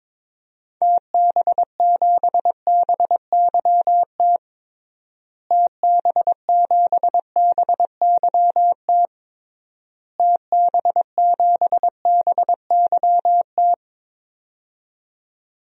Morse code, heard as a keyed tone, about 22 wpm, 700 Hz. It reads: TB7BYT TB7BYT TB7BYT
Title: DXCC-100 Call Signs - 6 Characters - Repeated 3x - 22wpm